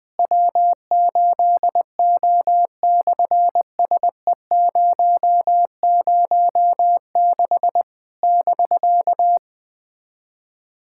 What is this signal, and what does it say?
Morse code, heard as a keyed tone, about 20 wpm, 700 Hz. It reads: W8O/SE006 <BK>
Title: SOTA QSO Practice 20wpm